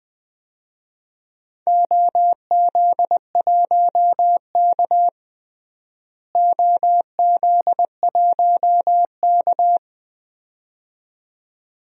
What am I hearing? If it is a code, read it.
Morse code, heard as a keyed tone, about 20 wpm, 700 Hz. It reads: OZ1K OZ1K
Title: DXCC-100 Call Signs - 4 Characters - Repeated 2x - 20wpm